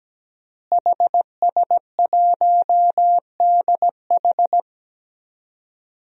Morse code, heard as a keyed tone, about 17 wpm, 700 Hz. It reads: HS1DH
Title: DXCC-100 Call Signs - 5 Characters - Repeated 2x - 17wpm